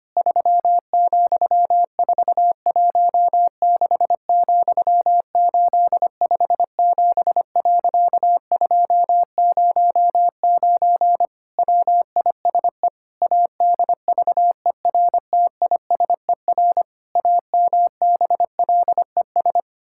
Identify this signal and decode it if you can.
Morse code, heard as a keyed tone, about 25 wpm, 700 Hz. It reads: 3,416,857.209 WISE ADVERTISER AMBLES
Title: Pangram Challenge v2 - 25wpm